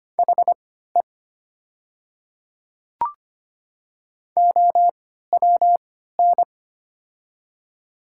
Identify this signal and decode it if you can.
Morse code, heard as a keyed tone, about 25 wpm, spaced as for 14 wpm, 700 Hz. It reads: HE OWN
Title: 3x Character Spacing - Top 500 Words - 25wpm